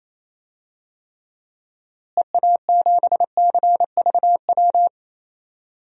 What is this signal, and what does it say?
Morse code, heard as a keyed tone, about 27 wpm, 700 Hz. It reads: EA7CVW
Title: DXCC-100 Call Signs - 6 Characters - Repeated 3x - 28wpm